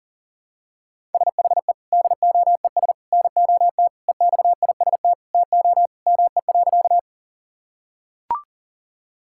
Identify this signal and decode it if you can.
Morse code, heard as a keyed tone, about 40 wpm, 700 Hz. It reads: SHE DOES NOT EXIST TO ME.
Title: Sentences from Top 1000 Words - Review Part 5 - 40wpm